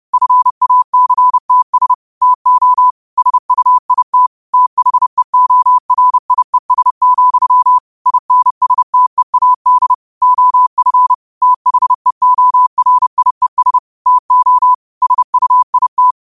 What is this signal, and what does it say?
Morse code, heard as a keyed tone, about 30 wpm, 1000 Hz. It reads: FACTS TO SUIT THEORIES, INSTEAD OF THEORIES TO SUIT